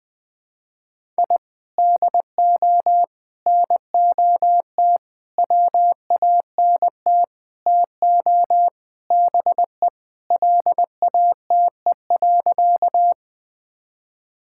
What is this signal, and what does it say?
Morse code, heard as a keyed tone, about 20 wpm, 700 Hz. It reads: I DO NOT WANT TO BE LATE.
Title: Sentences with 4-Word Phrases - 20wpm